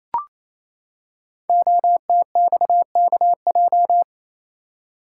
Morse code, heard as a keyed tone, about 27 wpm, 700 Hz. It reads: OTXKJ
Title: Code Groups - 5 Letter:Numbers - 28wpm